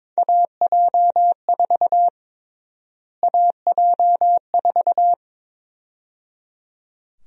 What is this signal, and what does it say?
Morse code, heard as a keyed tone, about 22 wpm, 700 Hz. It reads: AJ4 AJ4
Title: USA Call Sign Prefixes - Repeated 2x - 22wpm